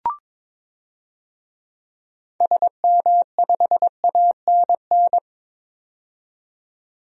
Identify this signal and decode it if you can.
Morse code, heard as a keyed tone, about 22 wpm, 700 Hz. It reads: SM5ANN
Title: DXCC-100 Call Signs - 6 Characters - 22wpm